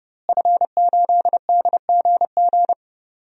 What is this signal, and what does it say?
Morse code, heard as a keyed tone, about 30 wpm, 700 Hz. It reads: F8DGG
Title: DXCC-100 Call Signs - 5 Characters - Repeated 3x - 30wpm